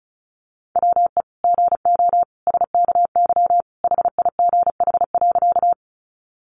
Morse code, heard as a keyed tone, about 34 wpm, 700 Hz. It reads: WE GO SKY HIGH.
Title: Sentences from Top 1200 Words 35wpm